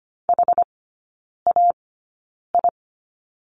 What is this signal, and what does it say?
Morse code, heard as a keyed tone, about 25 wpm, spaced as for 9 wpm, 690 Hz. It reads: HAI
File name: Lesson 15e - Number 5 - Three Characters 25wpm